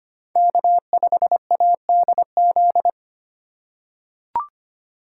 Morse code, heard as a keyed tone, about 25 wpm, 700 Hz. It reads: K5ADZ
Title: USA Call Signs - 5 Characters - 4x Character Spacing - 25wpm